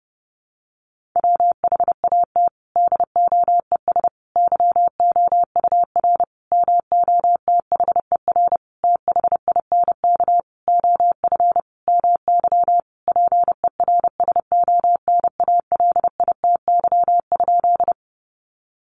Morse code, heard as a keyed tone, about 30 wpm, 700 Hz. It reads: WHAT DOES YOUR MOTHER THINK OF MY PERSONALITY?
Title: Sentences from Top 1400 Words 30wpm